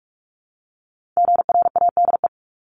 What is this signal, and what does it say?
Morse code, heard as a keyed tone, about 46 wpm, 700 Hz. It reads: GRADE
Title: Multi-letter Phonemes 45wpm